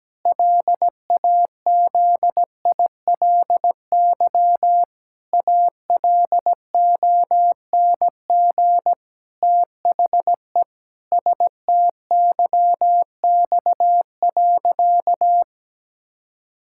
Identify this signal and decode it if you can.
Morse code, heard as a keyed tone, about 17 wpm, 700 Hz. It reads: LAZILY ALONG THE STYX.
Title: Pangram Challenge v2 - 17wpm